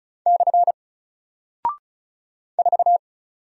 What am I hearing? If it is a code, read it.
Morse code, heard as a keyed tone, about 36 wpm, 700 Hz. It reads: / 4
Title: Single Character (Letter:Number:Punctuation:Prosign) Alt 35wpm